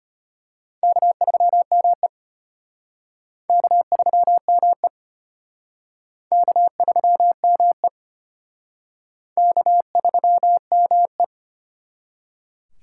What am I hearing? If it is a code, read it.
Morse code, heard as a keyed tone, about 33 wpm, 700 Hz. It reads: K3ME K3ME K3ME K3ME